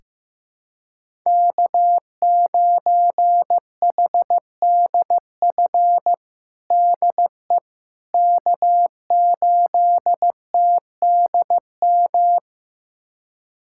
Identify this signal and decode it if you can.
Morse code, heard as a keyed tone, about 15 wpm, 700 Hz. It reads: K9HDF DE K8TDM
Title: USA Callsign DE Callsign - 5 Characters - 15wpm